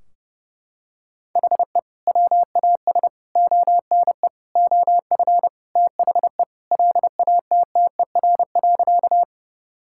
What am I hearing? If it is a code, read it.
Morse code, heard as a keyed tone, about 30 wpm, 700 Hz. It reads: HE WAS ONE OF THE LATTER.
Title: Sentences from Top 1600 Words 30wpm